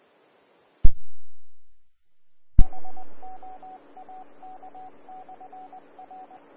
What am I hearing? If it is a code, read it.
Morse code, heard as a keyed tone, about 20 wpm, 750 Hz. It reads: HOAK/R
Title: Austin, TX Amateur Repeaters